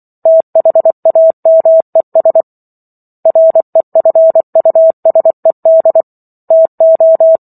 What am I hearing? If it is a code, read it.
Morse code, heard as a keyed tone, about 24 wpm, 650 Hz. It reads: THAMES REFUSED TO
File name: Chapter 15 - 24 WPM